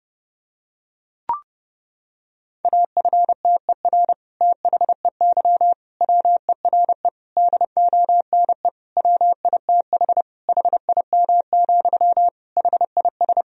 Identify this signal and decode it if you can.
Morse code, heard as a keyed tone, about 32 wpm, 700 Hz. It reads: AFTER THEY WERE DONE WITH HIM, HIS
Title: Sentences from Top 1000 Words - Review Part 2 - 30wpm